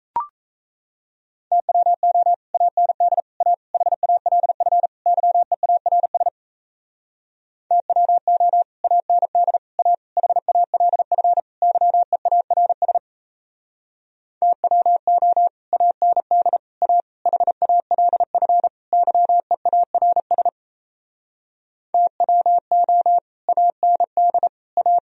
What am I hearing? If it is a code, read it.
Morse code, heard as a keyed tone, about 37 wpm, 700 Hz. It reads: TWO AND A HALF YEARS TWO AND A HALF YEARS TWO AND A HALF YEARS TWO AND A